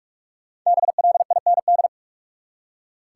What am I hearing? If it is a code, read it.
Morse code, heard as a keyed tone, about 46 wpm, 700 Hz. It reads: BLIND
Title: Five Letter Words - 45wpm